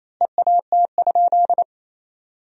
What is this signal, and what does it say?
Morse code, heard as a keyed tone, about 28 wpm, 700 Hz. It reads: EAT?
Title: 2x Word-Spacing - Sentences from Top 500 Words - 28wpm